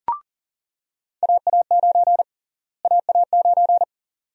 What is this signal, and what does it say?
Morse code, heard as a keyed tone, about 40 wpm, 700 Hz. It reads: AA9 AA9
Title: USA Call Sign Prefixes - Repeated 2x - 40wpm